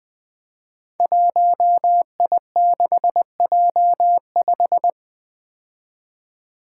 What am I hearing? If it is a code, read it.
Morse code, heard as a keyed tone, about 20 wpm, 700 Hz. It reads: 1I6J5
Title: Five Challenging Letter-Number Combinations 20wpm